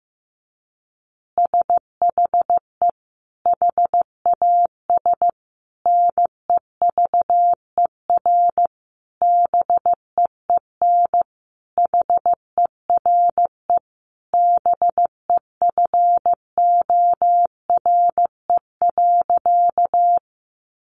Morse code, heard as a keyed tone, about 15 wpm, 700 Hz. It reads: SHE HAS NEVER BEEN HERE BEFORE.
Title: Additional Easy Phrases 15wpm